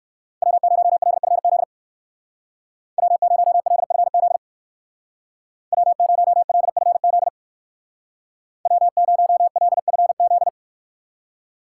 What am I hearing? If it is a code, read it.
Morse code, heard as a keyed tone, about 56 wpm, 700 Hz. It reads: W0LFZ W0LFZ W0LFZ W0LFZ
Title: Speed Racing - USA 1x3 Call Signs 45wpm